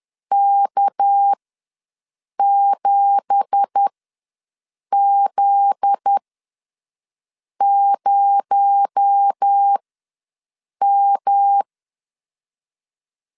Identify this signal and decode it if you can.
Morse code, heard as a keyed tone, about 10 wpm, spaced as for 6 wpm, 800 Hz. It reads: K7Z0M
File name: LEZIONI / 9 2867 ws 10wpm 4215